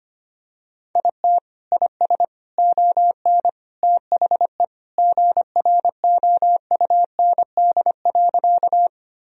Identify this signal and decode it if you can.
Morse code, heard as a keyed tone, about 25 wpm, 700 Hz. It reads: IT IS ON THE GROUND.
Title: Sentences from Top 1000 Words - Review Part 2 - 25wpm